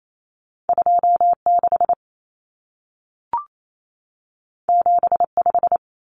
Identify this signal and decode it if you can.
Morse code, heard as a keyed tone, about 28 wpm, 700 Hz. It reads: 26 75
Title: Numbers - 2-Digit 28wpm